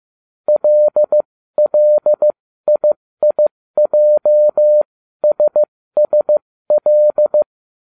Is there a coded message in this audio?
Yes: Morse code from 0.5 to 7.4 s, heard as a keyed tone, about 15 wpm, spaced as for 13 wpm, 600 Hz. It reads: LLIIJSSL